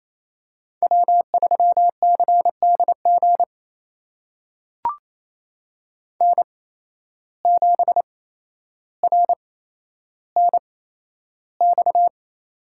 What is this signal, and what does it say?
Morse code, heard as a keyed tone, about 28 wpm, spaced as for 12 wpm, 700 Hz. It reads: W3CDG N7RNX
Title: USA Call Signs - 8x Character Spacing - 28wpm